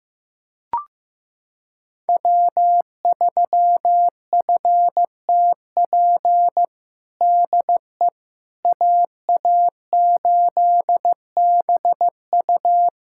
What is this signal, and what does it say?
Morse code, heard as a keyed tone, about 15 wpm, 700 Hz. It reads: W3FTP DE AA8BU
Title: USA Callsign DE Callsign - 5 Characters - 15wpm